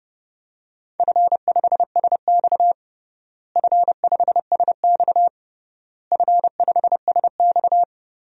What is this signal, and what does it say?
Morse code, heard as a keyed tone, about 30 wpm, 700 Hz. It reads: F5SX F5SX F5SX
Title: DXCC-100 Call Signs - 4 Characters - Repeated 3x - 30wpm